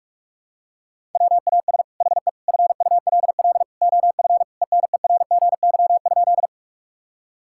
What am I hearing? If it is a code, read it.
Morse code, heard as a keyed tone, about 46 wpm, 700 Hz. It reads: WAS HE FULL OF ENERGY?
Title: Sentences from Top 1000 Words - Review Part 6 - 45wpm